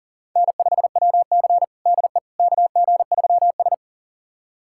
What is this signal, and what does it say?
Morse code, heard as a keyed tone, about 40 wpm, 700 Hz. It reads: N5WC DE KG3S
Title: USA Callsign DE Callsign - 4 Characters - 40wpm